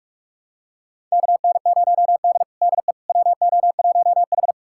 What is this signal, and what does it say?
Morse code, heard as a keyed tone, about 46 wpm, 700 Hz. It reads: KN0D DE WO1H
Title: USA Callsign DE Callsign - 4 Characters - 45wpm